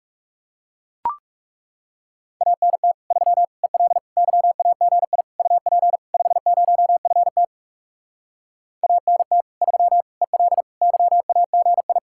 Morse code, heard as a keyed tone, about 40 wpm, 700 Hz. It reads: ANT 3 EL YAGI UP 50FT ANT 3 EL YAGI